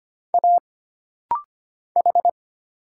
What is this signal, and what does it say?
Morse code, heard as a keyed tone, about 25 wpm, 700 Hz. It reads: A H
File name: Single Character (Letter:Number:Punctuation:Prosign) - Rapid-Fire - 25wpm